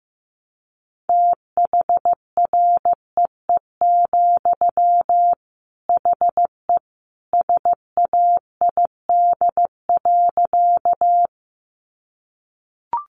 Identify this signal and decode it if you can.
Morse code, heard as a keyed tone, about 15 wpm, 700 Hz. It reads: THREE, HE SAID.
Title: Sentences from Top 200 Words 15wpm